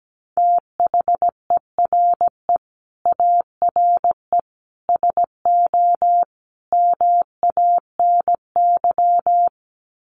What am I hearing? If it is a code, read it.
Morse code, heard as a keyed tone, about 17 wpm, 700 Hz. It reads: THERE ARE SO MANY
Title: Sets of 4 Words 17wpm